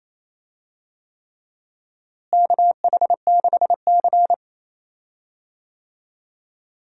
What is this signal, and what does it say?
Morse code, heard as a keyed tone, about 28 wpm, 700 Hz. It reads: KH6C
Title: DXCC-100 Call Signs - 4 Characters - Repeated 3x - 28wpm